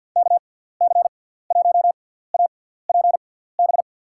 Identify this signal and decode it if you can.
Morse code, heard as a keyed tone, about 52 wpm, spaced as for 17 wpm, 700 Hz. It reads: KC1APB
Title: USA Call Signs - 6 Characters - 6x Character Spacing - 50wpm